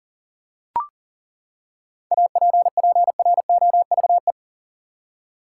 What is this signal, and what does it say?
Morse code, heard as a keyed tone, about 40 wpm, 700 Hz. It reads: APPROVE
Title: Seven Letter Words - 40wpm